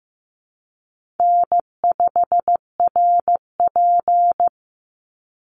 Morse code, heard as a keyed tone, about 15 wpm, 700 Hz. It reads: N5RP
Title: Easy Call Signs - 15wpm